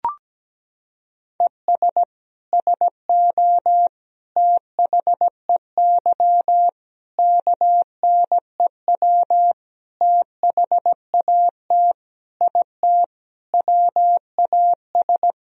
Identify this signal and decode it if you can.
Morse code, heard as a keyed tone, about 17 wpm, 700 Hz. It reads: ES SO THEY KNEW THAT IT WAS
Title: Sentences from Top 400 Words and Abbreviations 17wpm